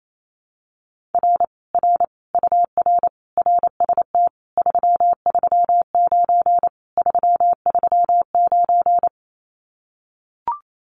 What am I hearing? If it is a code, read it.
Morse code, heard as a keyed tone, about 28 wpm, 700 Hz. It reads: R R UR RST 339 339
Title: Common QSO Phrases Encore 28wpm